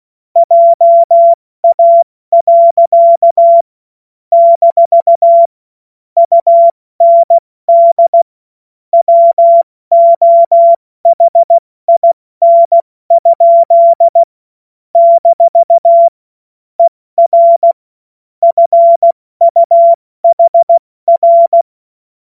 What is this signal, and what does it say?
Morse code, heard as a keyed tone, about 16 wpm, spaced as for 14 wpm, 680 Hz. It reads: JA. - UND WOHIN? - ER FUHR
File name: Morsekurs / Witze / Witz 12 März 2024 70 F80